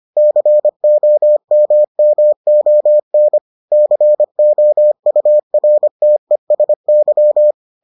Morse code, heard as a keyed tone, about 25 wpm, 600 Hz. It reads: COMMON COURTESY